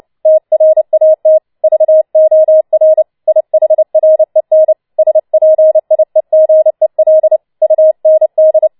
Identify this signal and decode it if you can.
Morse code, heard as a keyed tone, about 29 wpm, 630 Hz. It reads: TRAT VOR IHREN SPIEGEL UND
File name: Schneewittchen Teil3 30wpm